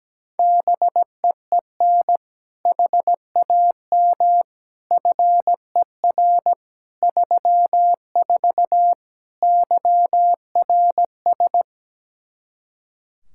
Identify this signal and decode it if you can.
Morse code, heard as a keyed tone, about 17 wpm, 700 Hz. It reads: BEEN HAM FER 34 YRS
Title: Practice QSO - Exchange 3 - More Info - 17wpm